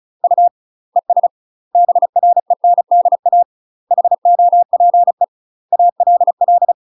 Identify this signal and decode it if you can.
Morse code, heard as a keyed tone, about 35 wpm, 700 Hz. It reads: U ES BRENDA HOPE ALL
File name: UK MASTER QSOs TRAINING / 35WPM / UK MASTER QSOs 0018